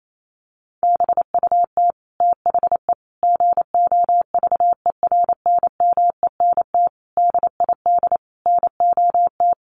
Morse code, heard as a keyed tone, about 28 wpm, 700 Hz. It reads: BUT THE GOVERNMENT DID NOT
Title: Sentences from Top 1400 Words 28wpm